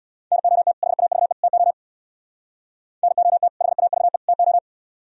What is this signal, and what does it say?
Morse code, heard as a keyed tone, about 60 wpm, 700 Hz. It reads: NOT HAVE TO NOT HAVE TO
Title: Speed-Racing - Sets of 3 Words 50wpm